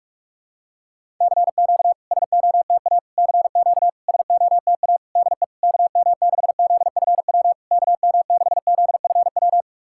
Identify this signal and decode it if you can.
Morse code, heard as a keyed tone, about 46 wpm, 690 Hz. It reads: CQ SOTA CQ SOTA DE KM6ZFW KM6ZFW
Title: SOTA QSO Practice 45wpm